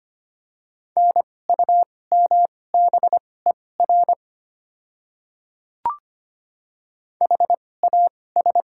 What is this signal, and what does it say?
Morse code, heard as a keyed tone, about 25 wpm, spaced as for 18 wpm, 700 Hz. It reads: NUMBER HAS
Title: 2x Character Spacing - Top 100 Words - 25wpm